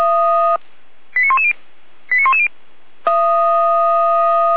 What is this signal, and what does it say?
Five-tone selective call (ZVEI), heard as quick runs of five tones, tones of about 80 ms. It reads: 89110 89110